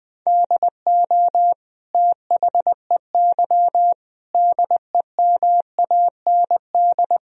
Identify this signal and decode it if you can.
Morse code, heard as a keyed tone, about 20 wpm, 700 Hz. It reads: DO THEY DEMAND
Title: Sentences from Top 1000 Words - Review Part 3 - 20wpm